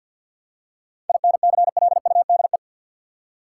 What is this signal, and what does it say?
Morse code, heard as a keyed tone, about 52 wpm, 700 Hz. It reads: INCLUDE
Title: Seven Letter Words - 50wpm